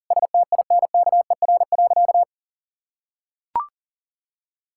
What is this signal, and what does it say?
Morse code, heard as a keyed tone, about 40 wpm, 700 Hz. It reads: STINKER.
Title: Pangram Challenge v2 - 40wpm